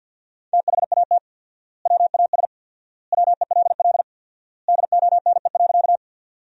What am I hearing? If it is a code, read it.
Morse code, heard as a keyed tone, about 52 wpm, 700 Hz. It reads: THAT WAS WELL DONE.
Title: 4x Word-Spacing - Sentences from Top 500 Words - 50wpm